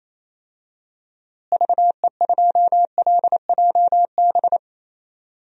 Morse code, heard as a keyed tone, about 28 wpm, 700 Hz. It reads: VE2LJB
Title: DXCC-100 Call Signs - 6 Characters - Repeated 2x - 28wpm